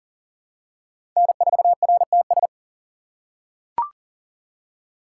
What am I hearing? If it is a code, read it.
Morse code, heard as a keyed tone, about 40 wpm, 700 Hz. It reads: N4RTS
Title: USA Call Signs - 5 Characters - 2x Character Spacing - 40wpm